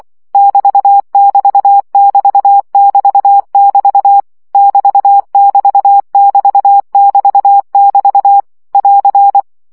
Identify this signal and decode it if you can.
Morse code, heard as a keyed tone, about 24 wpm, 800 Hz. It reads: <BT><BT><BT><BT><BT> <BT><BT><BT><BT><BT> <AR>
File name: CW gyakorlas egyenlo 20 baud